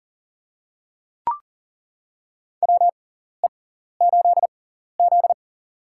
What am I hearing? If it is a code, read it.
Morse code, heard as a keyed tone, about 38 wpm, spaced as for 14 wpm, 700 Hz. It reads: WE8Z
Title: USA Call Signs - 4 Characters - 6x Character Spacing - 40wpm